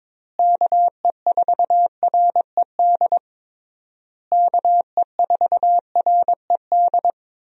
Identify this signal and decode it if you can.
Morse code, heard as a keyed tone, about 22 wpm, 700 Hz. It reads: KE4RED KE4RED
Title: USA Call Signs - 6 Characters - Repeated 3x - 22wpm